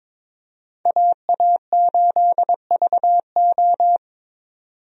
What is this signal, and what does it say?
Morse code, heard as a keyed tone, about 21 wpm, 700 Hz. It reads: AA8VO
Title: USA Call Signs - 5 Characters - 22wpm